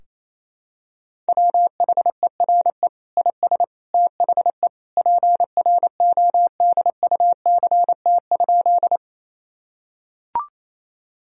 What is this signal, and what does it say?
Morse code, heard as a keyed tone, about 28 wpm, 700 Hz. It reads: WHERE IS THE PRODUCT?
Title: Sentences from Top 1000 Words - Review Part 2 - 28wpm